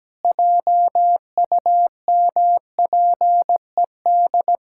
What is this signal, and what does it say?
Morse code, heard as a keyed tone, about 17 wpm, 700 Hz. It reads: JUMPED
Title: Pangram Challenge v2 - 17wpm